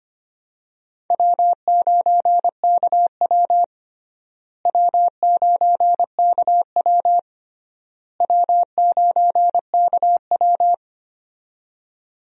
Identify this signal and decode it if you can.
Morse code, heard as a keyed tone, about 25 wpm, 700 Hz. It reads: W9KW W9KW W9KW
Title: USA Call Signs - 4 Characters - Repeated 3x - 25wpm